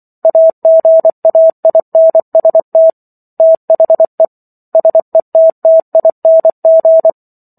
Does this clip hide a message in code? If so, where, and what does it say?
Yes: Morse code from 0.2 to 7.1 s, heard as a keyed tone, about 24 wpm, 650 Hz. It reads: AGAINST THE SETTING